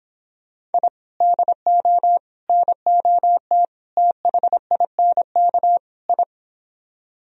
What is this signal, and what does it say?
Morse code, heard as a keyed tone, about 31 wpm, 700 Hz. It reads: I DO NOT THINK I